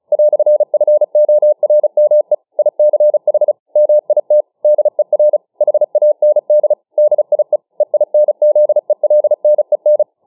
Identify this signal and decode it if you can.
Morse code, heard as a keyed tone, about 36 wpm, 600 Hz. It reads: "FORME ICH MIT DER HAND DIE EINZELNEN